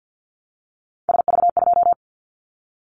Morse code, heard as a keyed tone, about 50 wpm, 700 Hz. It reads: SV2
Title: Callsign Prefixes for Top 50 DXCC - 50wpm